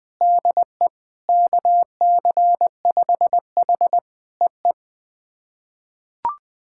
Morse code, heard as a keyed tone, about 20 wpm, 690 Hz. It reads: DE KC5H EE
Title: POTA QSO Practice - 20wpm